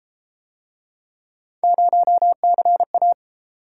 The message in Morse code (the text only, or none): 0CA